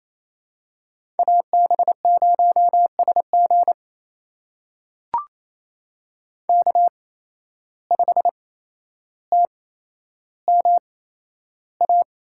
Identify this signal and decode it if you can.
Morse code, heard as a keyed tone, about 28 wpm, spaced as for 12 wpm, 700 Hz. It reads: AB0SG K5TMA
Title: USA Call Signs - 5 Characters - 8x Character Spacing - 28wpm